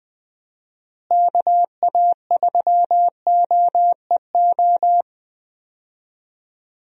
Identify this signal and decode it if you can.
Morse code, heard as a keyed tone, about 20 wpm, 700 Hz. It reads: KA3OEO
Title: USA Call Signs - 6 Characters - Repeated 2x - 20wpm